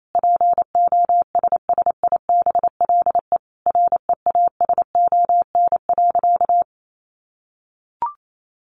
Morse code, heard as a keyed tone, about 28 wpm, 700 Hz. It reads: POSSIBLE REASON.
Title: Sentences from Top 700 Words 28wpm